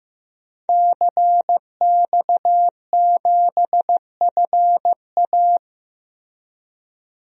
Morse code, heard as a keyed tone, about 15 wpm, 700 Hz. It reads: CX7FA
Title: DXCC-100 Call Signs - 5 Characters - Repeated 3x - 15wpm